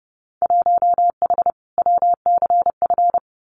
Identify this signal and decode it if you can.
Morse code, heard as a keyed tone, about 30 wpm, 700 Hz. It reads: 1H WCF
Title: Winter Field Day Exchanges 30wpm